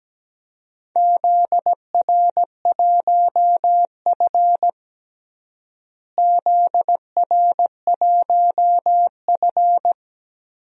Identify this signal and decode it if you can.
Morse code, heard as a keyed tone, about 17 wpm, 700 Hz. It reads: ZR1F ZR1F
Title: DXCC-100 Call Signs - 4 Characters - Repeated 2x - 17wpm